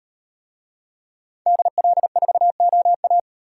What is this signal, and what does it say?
Morse code, heard as a keyed tone, about 37 wpm, 700 Hz. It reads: DL4OA